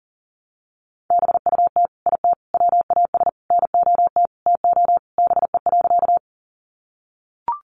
Morse code, heard as a keyed tone, about 40 wpm, 700 Hz. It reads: BUT IT WAS NOT TO BE.
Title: Sentences from Top 600 Words 40wpm